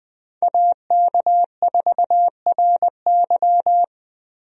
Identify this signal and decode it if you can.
Morse code, heard as a keyed tone, about 20 wpm, 700 Hz. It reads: AK4RY
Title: USA Call Signs - 5 Characters - Repeated 3x - 20wpm